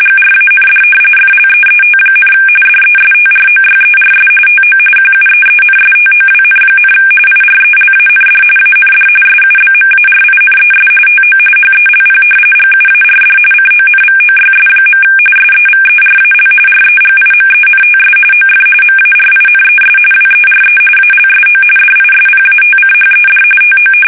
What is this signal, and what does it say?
RTTY, about 45.45 baud. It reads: GRAND FLEET, DESTROYERS TO QUEENSTOWN, IRELAND AND SUBMARINES TO HELP GUARD CONVOYS. SEVERAL REGIMENTS OF MARINES WERE ALSO DISPATCHED TO
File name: fsk850